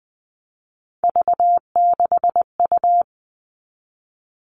Morse code, heard as a keyed tone, about 20 wpm, 700 Hz. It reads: V6U
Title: Three Challenging Letter-Number Combinations 20wpm